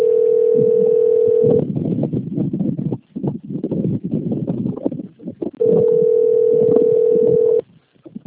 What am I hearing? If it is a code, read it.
Ringback tone (North America), 440 + 480 Hz.